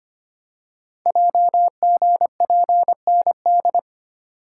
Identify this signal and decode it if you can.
Morse code, heard as a keyed tone, about 25 wpm, 700 Hz. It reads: JGPND
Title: Code Groups - 5 Letters - 25wpm